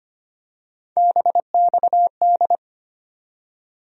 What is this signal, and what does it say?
Morse code, heard as a keyed tone, about 25 wpm, 700 Hz. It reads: BXD
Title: USA Call Sign Postfixes - 3 Letters 25wpm